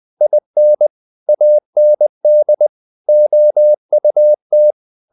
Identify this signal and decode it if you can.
Morse code, heard as a keyed tone, about 20 wpm, 600 Hz. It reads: IN AND OUT